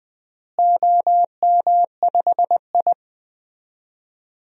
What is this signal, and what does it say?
Morse code, heard as a keyed tone, about 20 wpm, 700 Hz. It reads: OM5I
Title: DXCC-100 Call Signs - 4 Characters - Repeated 3x - 20wpm